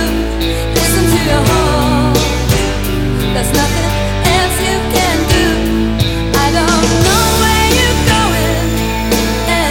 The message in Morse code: none